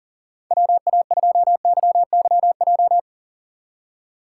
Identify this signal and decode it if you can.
Morse code, heard as a keyed tone, about 40 wpm, 700 Hz. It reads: WA2YYJ
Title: USA Call Signs - 6 Characters - 6x Character Spacing - 40wpm